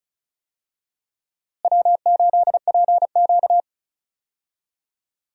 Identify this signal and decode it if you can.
Morse code, heard as a keyed tone, about 34 wpm, 700 Hz. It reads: W8PQ
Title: USA Call Signs - 4 Characters - Repeated 2x - 35wpm